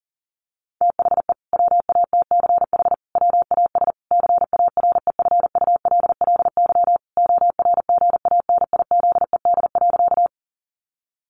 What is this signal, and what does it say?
Morse code, heard as a keyed tone, about 40 wpm, 700 Hz. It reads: THE WATCH WAS CAREFULLY ORGANIZED.
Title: Sentences from Top 1300 Words 40wpm